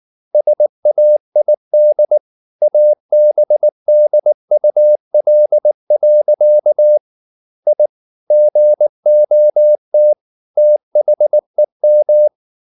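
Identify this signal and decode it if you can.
Morse code, heard as a keyed tone, about 19 wpm, 600 Hz. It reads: SAID ABDUL. I GOT THEM
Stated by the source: Morse code at 19 wpm from CW med Gnister, reading Ken Follett's novel Never in English.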